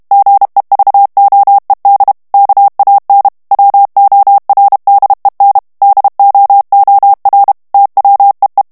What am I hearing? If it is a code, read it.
Morse code, heard as a keyed tone, about 32 wpm, 800 Hz. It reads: GEVOED KAN WORDEN DOOR TWEE
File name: HOLLANDAIS texte clair 1 vitesse 30 mots minute